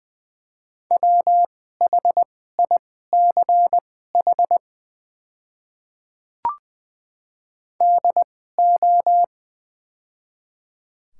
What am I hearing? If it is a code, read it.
Morse code, heard as a keyed tone, about 20 wpm, spaced as for 14 wpm, 700 Hz. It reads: WHICH DO
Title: 2x Character Spacing - Top 100 Words - 20wpm